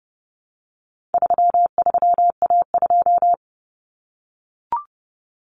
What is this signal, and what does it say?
Morse code, heard as a keyed tone, about 30 wpm, 700 Hz. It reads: 33A2
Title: Common Cut Numbers - 4-Digit 30wpm